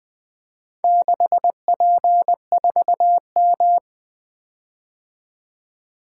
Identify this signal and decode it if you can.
Morse code, heard as a keyed tone, about 20 wpm, 700 Hz. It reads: 6P4M
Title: Four Letter-Number Combinations - 20wpm